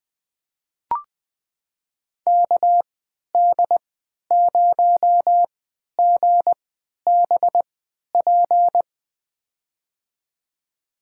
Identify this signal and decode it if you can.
Morse code, heard as a keyed tone, about 20 wpm, spaced as for 11 wpm, 700 Hz. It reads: KD0GBP
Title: USA Call Signs - 6 Characters - 3x Character Spacing - 20wpm